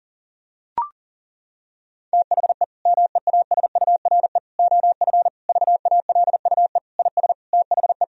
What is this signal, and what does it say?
Morse code, heard as a keyed tone, about 40 wpm, 700 Hz. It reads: THE MEASURE OF VALUE IS THE
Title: Sentences from Top 1000 Words - Review Part 3 - 40wpm